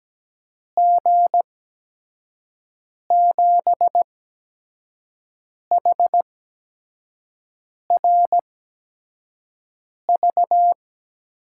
Morse code, heard as a keyed tone, about 17 wpm, spaced as for 5 wpm, 700 Hz. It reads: G7HRV
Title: USA Call Signs - 6 Characters - 8x Character Spacing - 17wpm